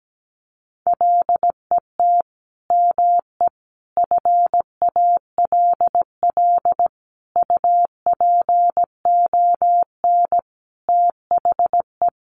Morse code, heard as a keyed tone, about 17 wpm, 700 Hz. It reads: LET ME FALL UPON THE
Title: Sentences from Top 1100 Words 17wpm